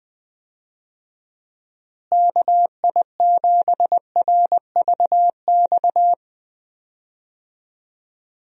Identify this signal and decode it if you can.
Morse code, heard as a keyed tone, about 21 wpm, 700 Hz. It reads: KI7RVX
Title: USA Call Signs - 6 Characters - Repeated 2x - 20wpm